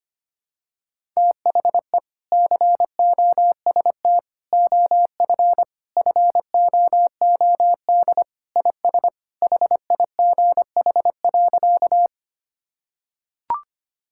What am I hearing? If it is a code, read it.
Morse code, heard as a keyed tone, about 25 wpm, 700 Hz. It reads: THE COST OF FOOD IS HIGH.
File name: practice / 25wpm / Sentences with 3-Word Phrases - 25wpm